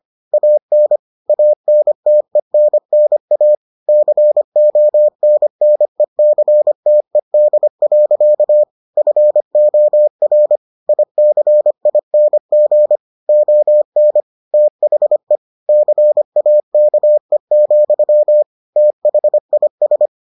Morse code, heard as a keyed tone, about 25 wpm, 600 Hz. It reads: AN ANTENNA CONNECTED. FOR ICING ON THE CAKE, THIS